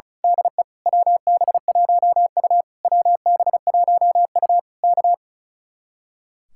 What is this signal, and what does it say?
Morse code, heard as a keyed tone, about 32 wpm, 700 Hz. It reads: DE WB1U WB1U K
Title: Speed-Racing Common QSO Phrases - 35wpm